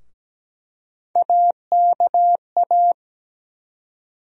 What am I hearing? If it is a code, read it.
Morse code, heard as a keyed tone, about 17 wpm, 700 Hz. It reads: AKA
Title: USA Call Sign Postfixes - 3 Letters 17wpm